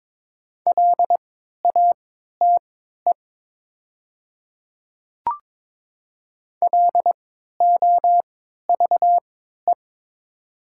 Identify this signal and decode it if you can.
Morse code, heard as a keyed tone, about 21 wpm, spaced as for 12 wpm, 700 Hz. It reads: LATE LOVE
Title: 3x Character Spacing - Top 500 Words - 22wpm